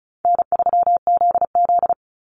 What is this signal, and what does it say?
Morse code, heard as a keyed tone, about 34 wpm, 700 Hz. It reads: N3ZZ
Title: Call Signs Repeated 3x - 35wpm